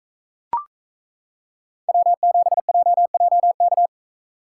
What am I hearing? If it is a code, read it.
Morse code, heard as a keyed tone, about 34 wpm, 700 Hz. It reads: W7JJK
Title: Speed Racing - USA 1x3 Call Signs 28wpm